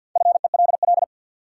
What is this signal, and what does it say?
Morse code, heard as a keyed tone, about 52 wpm, 700 Hz. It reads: FELL
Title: Sentences from Top 1600 Words - New Word Drill - 50wpm